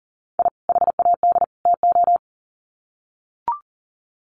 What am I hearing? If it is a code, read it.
Morse code, heard as a keyed tone, about 40 wpm, 700 Hz. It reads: I HAD TO
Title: Sets of 3 Words 40wpm